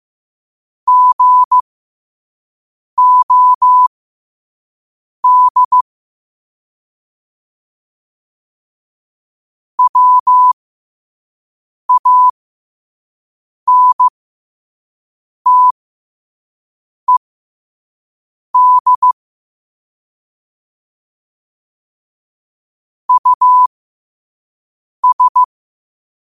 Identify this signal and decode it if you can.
Morse code, heard as a keyed tone, about 15 wpm, spaced as for 5 wpm, 1000 Hz. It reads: GOD WANTED US
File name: qotdfiles / 5wpm